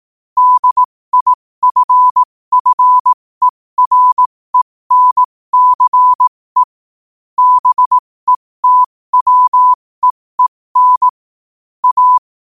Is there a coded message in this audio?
Yes: Morse code from 0.4 to 12.2 s, heard as a keyed tone, about 18 wpm, spaced as for 15 wpm, 1000 Hz. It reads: DIFFERENCE BETWEEN A